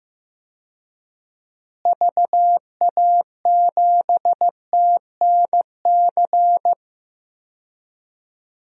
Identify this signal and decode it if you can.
Morse code, heard as a keyed tone, about 14 wpm, 700 Hz. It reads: VA7TNC
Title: DXCC-100 Call Signs - 6 Characters - Repeated 3x - 15wpm